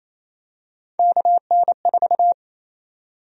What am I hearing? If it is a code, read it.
Morse code, heard as a keyed tone, about 28 wpm, 700 Hz. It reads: KN4
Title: USA Call Sign Prefixes - 28wpm